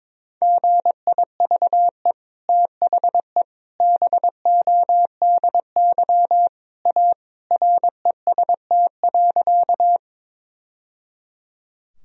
Morse code, heard as a keyed tone, about 22 wpm, 700 Hz. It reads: GIVE THE BODY A REST.
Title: Sentences from Top 1000 Words - Review Part 2 - 22wpm